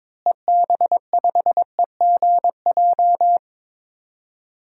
Morse code, heard as a keyed tone, about 22 wpm, 700 Hz. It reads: EB5EGJ
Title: DXCC-100 Call Signs - 6 Characters - Repeated 2x - 22wpm